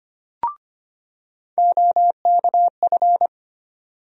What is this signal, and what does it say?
Morse code, heard as a keyed tone, about 25 wpm, 700 Hz. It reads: OKF
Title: USA Call Sign Postfixes - 3 Letters 25wpm